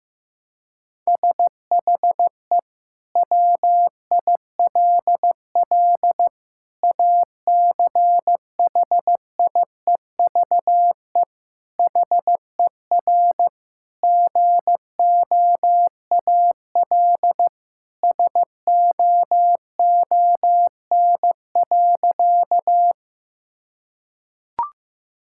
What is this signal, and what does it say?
Morse code, heard as a keyed tone, about 15 wpm, 700 Hz. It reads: SHE WILL ACHIEVE HER GOAL SOON.
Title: Sentences from Top 1000 Words - Review Part 6 - 15wpm